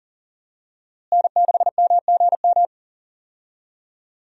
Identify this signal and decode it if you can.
Morse code, heard as a keyed tone, about 40 wpm, 700 Hz. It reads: N6MGM
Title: USA Call Signs - 5 Characters - Repeated 3x - 40wpm